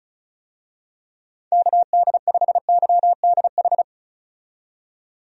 Morse code, heard as a keyed tone, about 36 wpm, 700 Hz. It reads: KD5YDH
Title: USA Call Signs - 6 Characters - 35wpm